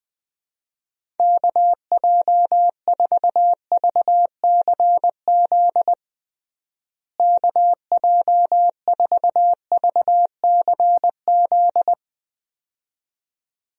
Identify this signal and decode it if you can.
Morse code, heard as a keyed tone, about 20 wpm, 700 Hz. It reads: KJ4VCZ KJ4VCZ
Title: USA Call Signs - 6 Characters - Repeated 2x - 20wpm